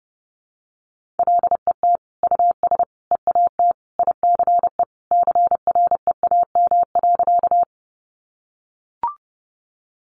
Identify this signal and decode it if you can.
Morse code, heard as a keyed tone, about 30 wpm, 700 Hz. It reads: LET US EAT ICE CREAM.
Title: Sentences from Top 1500 Words 30wpm